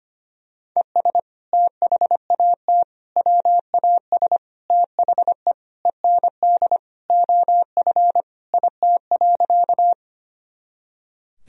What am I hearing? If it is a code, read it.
Morse code, heard as a keyed tone, about 25 wpm, 700 Hz. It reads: ES THAT WAS THE END OF IT.
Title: Sentences from Top 200 Words and Abbreviations 25wpm